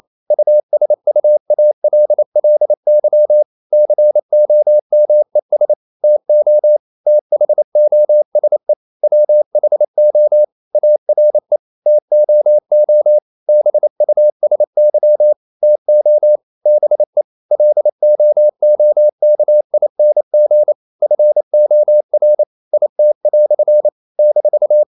Morse code, heard as a keyed tone, about 28 wpm, 600 Hz. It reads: USUALLY COMES TO THOSE WHO ARE TOO BUSY TO BE LOOKING FOR IT" -
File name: CW / Practice / EnglishQuotations0228